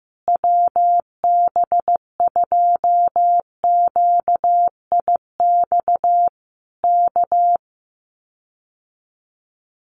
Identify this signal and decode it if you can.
Morse code, heard as a keyed tone, about 15 wpm, 700 Hz. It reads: WB2QIX K
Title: Common QSO Phrases Encore 15wpm